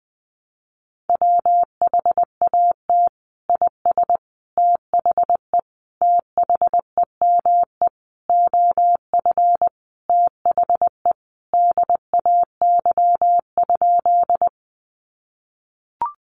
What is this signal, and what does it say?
Morse code, heard as a keyed tone, about 20 wpm, 700 Hz. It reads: WHAT IS THE THEME OF THE DAY?
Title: Sentences from Top 1100 Words 20wpm